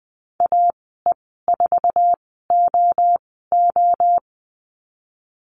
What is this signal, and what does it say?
Morse code, heard as a keyed tone, about 19 wpm, spaced as for 14 wpm, 700 Hz. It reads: AE4OO
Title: 062 Lesson 11g - Letter h - Call Signs 20wpm